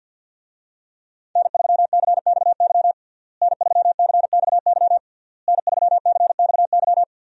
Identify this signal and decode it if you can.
Morse code, heard as a keyed tone, about 51 wpm, 700 Hz. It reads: N3CXY N3CXY N3CXY
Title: USA Call Signs - 5 Characters - Repeated 3x - 50wpm